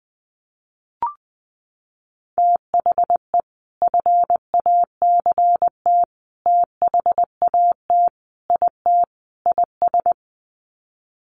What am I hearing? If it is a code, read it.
Morse code, heard as a keyed tone, about 20 wpm, 700 Hz. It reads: THE FACT THAT IT IS
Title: Sets of 5 Words Encore2 20wpm